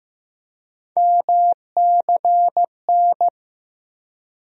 Morse code, heard as a keyed tone, about 15 wpm, 700 Hz. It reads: MCN
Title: USA Call Sign Postfixes - 3 Letters 15wpm